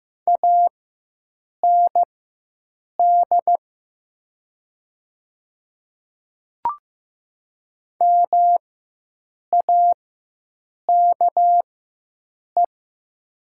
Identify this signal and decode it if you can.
Morse code, heard as a keyed tone, about 15 wpm, spaced as for 7 wpm, 700 Hz. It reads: AND MAKE
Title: 4x Character Spacing - Top 100 Words - 15wpm